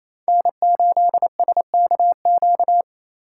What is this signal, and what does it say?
Morse code, heard as a keyed tone, about 28 wpm, 700 Hz. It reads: N8SKQ
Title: USA Call Signs - 2x Character Spacing - 28wpm